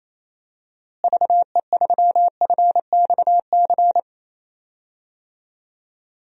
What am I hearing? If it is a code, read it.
Morse code, heard as a keyed tone, about 28 wpm, 700 Hz. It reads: VE3FXC
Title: DXCC-100 Call Signs - 6 Characters - Repeated 3x - 28wpm